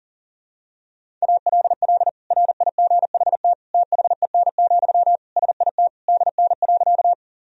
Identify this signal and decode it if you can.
Morse code, heard as a keyed tone, about 37 wpm, 700 Hz. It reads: ALL RIGHT THEN, SIT DN.
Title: Sentences from Top 500 Words and Abbreviations 40wpm